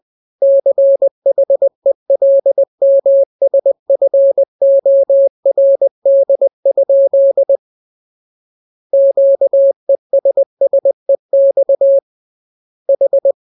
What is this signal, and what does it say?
Morse code, heard as a keyed tone, about 20 wpm, 550 Hz. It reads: CHELMSFORD? Q ESSEX H
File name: jokes / joke 25 20 0000